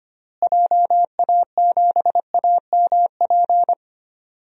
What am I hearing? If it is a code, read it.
Morse code, heard as a keyed tone, about 25 wpm, 700 Hz. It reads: JA7AMP
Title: DXCC-100 Call Signs - 6 Characters - Repeated 3x - 25wpm